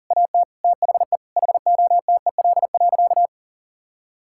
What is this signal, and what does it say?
Morse code, heard as a keyed tone, about 40 wpm, 700 Hz. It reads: AT THE HOTEL.
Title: Sentences from Top 1600 Words 40wpm